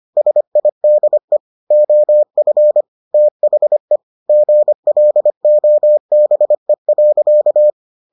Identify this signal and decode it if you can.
Morse code, heard as a keyed tone, about 25 wpm, 600 Hz. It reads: SIDE OF THE GLOBE.